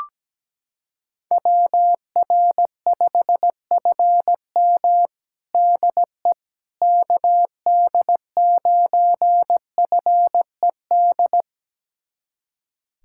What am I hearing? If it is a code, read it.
Morse code, heard as a keyed tone, about 17 wpm, 700 Hz. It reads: WR5FM DE KD9FED
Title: USA Callsign DE Callsign 17wpm